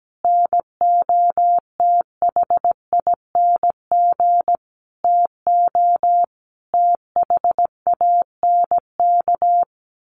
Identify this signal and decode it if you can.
Morse code, heard as a keyed tone, about 17 wpm, 700 Hz. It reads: NOTHING TO THANK
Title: Sentences from Top 1000 Words 17wpm